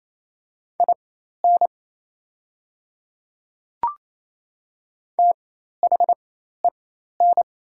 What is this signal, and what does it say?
Morse code, heard as a keyed tone, about 28 wpm, spaced as for 13 wpm, 700 Hz. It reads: IN THEN
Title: 4x Character Spacing - Top 100 Words - 28wpm